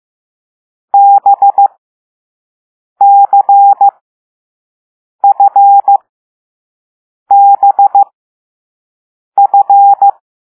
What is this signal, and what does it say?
Morse code, heard as a keyed tone, about 15 wpm, spaced as for 5 wpm, 800 Hz. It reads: BCFBF
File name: Morse / Level6B